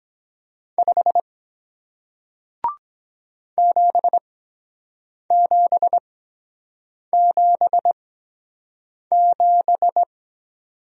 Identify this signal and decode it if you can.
Morse code, heard as a keyed tone, about 23 wpm, spaced as for 7 wpm, 700 Hz. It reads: 5 7777